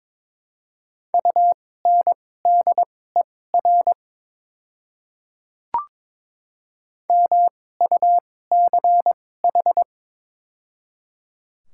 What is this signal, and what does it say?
Morse code, heard as a keyed tone, about 22 wpm, spaced as for 16 wpm, 700 Hz. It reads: UNDER MUCH
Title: 2x Character Spacing - Top 300 Words - 22wpm